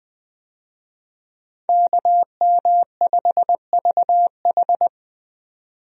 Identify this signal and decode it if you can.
Morse code, heard as a keyed tone, about 20 wpm, 700 Hz. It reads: KM5VH
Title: USA Call Signs - 5 Characters - 20wpm